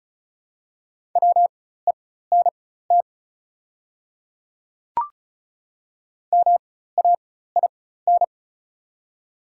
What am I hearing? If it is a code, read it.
Morse code, heard as a keyed tone, about 36 wpm, spaced as for 16 wpm, 690 Hz. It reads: WENT MAIN
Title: 4x Character Spacing - Top 500 Words - 35wpm